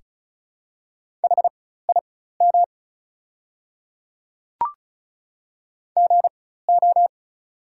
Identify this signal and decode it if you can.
Morse code, heard as a keyed tone, about 36 wpm, spaced as for 16 wpm, 700 Hz. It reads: HIM GO
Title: 4x Character Spacing - Top 100 Words - 35wpm